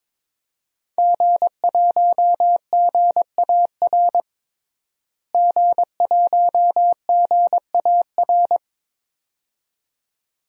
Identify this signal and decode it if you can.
Morse code, heard as a keyed tone, about 22 wpm, 700 Hz. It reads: G1GAR G1GAR
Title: DXCC-100 Call Signs - 5 Characters - Repeated 2x - 22wpm